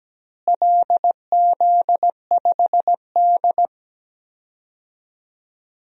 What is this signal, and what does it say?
Morse code, heard as a keyed tone, about 17 wpm, 700 Hz. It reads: LZ5D
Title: DXCC-100 Call Signs - 4 Characters - Repeated 2x - 17wpm